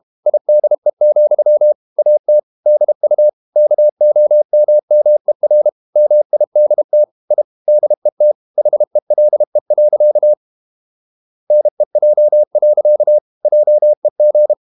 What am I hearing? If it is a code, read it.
Morse code, heard as a keyed tone, about 32 wpm, 600 Hz. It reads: IDE, AT DU KOMMER MIDT I DET HELE. NEJ. JEG
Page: Ferie 32 wpm